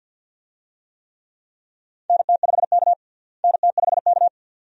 Morse code, heard as a keyed tone, about 52 wpm, 700 Hz. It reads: NT5K NT5K
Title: USA Call Signs - 4 Characters - Repeated 2x - 50wpm